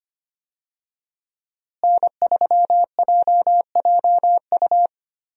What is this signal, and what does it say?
Morse code, heard as a keyed tone, about 25 wpm, 700 Hz. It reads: N3JJU
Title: USA Call Signs - 5 Characters - Repeated 2x - 25wpm